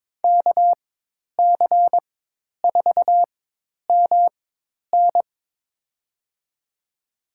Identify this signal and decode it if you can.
Morse code, heard as a keyed tone, about 21 wpm, spaced as for 10 wpm, 700 Hz. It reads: KC4MN
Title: USA Call Signs - 5 Characters - 4x Character Spacing - 22wpm